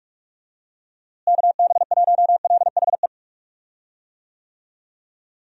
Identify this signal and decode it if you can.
Morse code, heard as a keyed tone, about 46 wpm, 700 Hz. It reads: KB1LHE
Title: USA Call Signs - 6 Characters - Repeated 2x - 45wpm